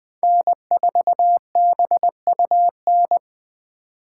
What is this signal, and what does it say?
Morse code, heard as a keyed tone, about 20 wpm, 700 Hz. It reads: N4BUN
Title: 2023 POTA Hunter Call Signs - 20wpm